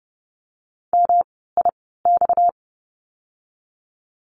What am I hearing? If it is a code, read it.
Morse code, heard as a keyed tone, about 30 wpm, spaced as for 17 wpm, 700 Hz. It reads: MIX
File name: course / 30:10wpm / 230 Lesson 37f - Letter x - Words 30wpm